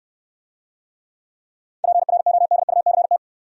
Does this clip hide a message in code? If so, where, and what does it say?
Yes: Morse code from 1.8 to 3.2 s, heard as a keyed tone, about 63 wpm, 700 Hz. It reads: PRODUCT